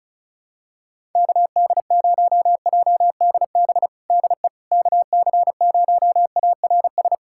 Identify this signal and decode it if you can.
Morse code, heard as a keyed tone, about 36 wpm, 700 Hz. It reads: KD0JDB DE KC0ARS
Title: USA Callsign DE Callsign - 6 Characters - 35wpm